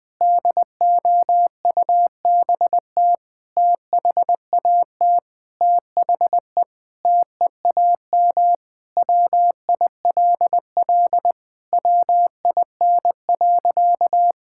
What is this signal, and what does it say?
Morse code, heard as a keyed tone, about 20 wpm, 700 Hz. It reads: DOUBT THAT THE TEAM WILL WIN.
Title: Sentences with 6-Word Phrases - 20wpm